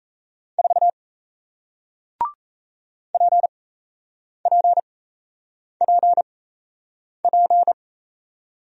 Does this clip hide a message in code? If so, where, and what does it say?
Yes: Morse code from 0.6 to 7.7 s, heard as a keyed tone, about 38 wpm, spaced as for 8 wpm, 700 Hz. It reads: 4 PPPP